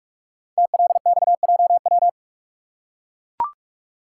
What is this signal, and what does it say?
Morse code, heard as a keyed tone, about 44 wpm, 700 Hz. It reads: TLXJW
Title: Code Groups - 5 Letters - 45wpm